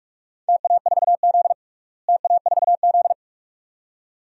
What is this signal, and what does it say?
Morse code, heard as a keyed tone, about 41 wpm, 700 Hz. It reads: TA4Z TA4Z
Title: DXCC-100 Call Signs - 4 Characters - Repeated 2x - 45wpm